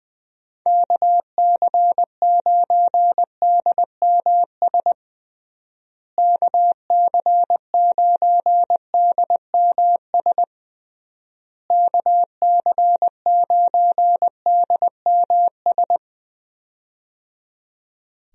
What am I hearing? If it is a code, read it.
Morse code, heard as a keyed tone, about 20 wpm, 700 Hz. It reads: KC9DMS KC9DMS KC9DMS
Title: USA Call Signs - 6 Characters - Repeated 3x - 20wpm